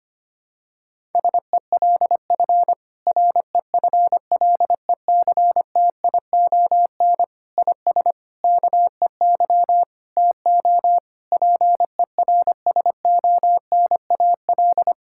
Morse code, heard as a keyed tone, about 25 wpm, 700 Hz. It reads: SELF REFLECTION IS KEY TO PERSONAL
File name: practice / 25wpm / Difficult Sentences with Hints - 25wpm